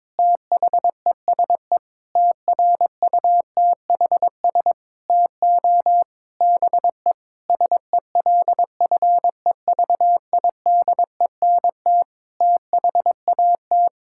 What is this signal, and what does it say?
Morse code, heard as a keyed tone, about 22 wpm, 700 Hz. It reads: THESE TRUTHS TO BE SELFEVIDENT THAT
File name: Sets of 9 Words - 22wpm